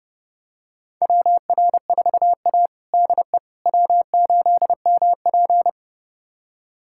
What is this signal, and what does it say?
Morse code, heard as a keyed tone, about 30 wpm, 700 Hz. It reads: WR4A DE W8MP
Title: USA Callsign DE Callsign - 4 Characters - 30wpm